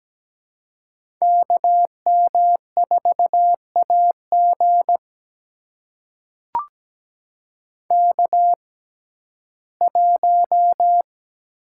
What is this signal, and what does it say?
Morse code, heard as a keyed tone, about 17 wpm, spaced as for 12 wpm, 700 Hz. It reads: KM4AG K1
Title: USA Call Signs - 5 Characters - 6x Character Spacing - 17wpm